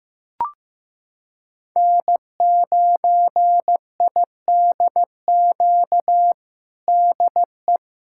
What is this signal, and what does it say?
Morse code, heard as a keyed tone, about 15 wpm, 700 Hz. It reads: N9IDQ DE
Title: USA Callsign DE Callsign 15wpm